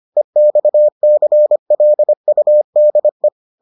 Morse code, heard as a keyed tone, about 25 wpm, 600 Hz. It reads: EXCLUDE